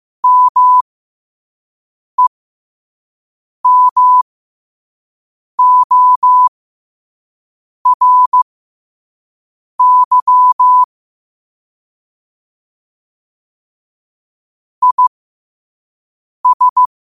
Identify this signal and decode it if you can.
Morse code, heard as a keyed tone, about 15 wpm, spaced as for 5 wpm, 1000 Hz. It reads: MEMORY IS
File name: qotdfiles / 5wpm